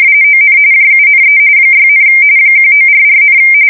RTTY, about 45.45 baud. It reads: ON JUNE 6, 1944,
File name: fsk170 / USS-New-Jersey